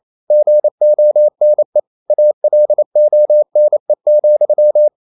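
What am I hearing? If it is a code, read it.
Morse code, heard as a keyed tone, about 28 wpm, 600 Hz. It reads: GONE ALONE,
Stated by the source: Morse code practice at 28 wpm, text of Ken Follett's novel Never in English, from CW med Gnister.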